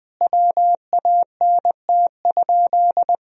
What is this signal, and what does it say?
Morse code, heard as a keyed tone, about 20 wpm, 700 Hz. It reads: WANT?
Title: Sentences from Top 200 Words and Abbreviations 20wpm